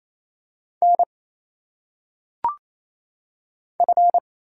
Morse code, heard as a keyed tone, about 27 wpm, 700 Hz. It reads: N F
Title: Single Letters - Phonetic Pronunciation 28wpm